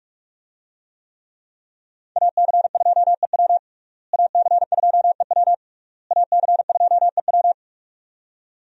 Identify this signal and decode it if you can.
Morse code, heard as a keyed tone, about 42 wpm, 700 Hz. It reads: AC2EW AC2EW AC2EW
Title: USA Call Signs - 5 Characters - Repeated 3x - 45wpm